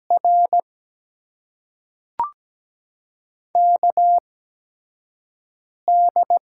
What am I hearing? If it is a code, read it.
Morse code, heard as a keyed tone, about 17 wpm, spaced as for 5 wpm, 700 Hz. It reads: R KD
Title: USA Call Signs - 6 Characters - 8x Character Spacing - 17wpm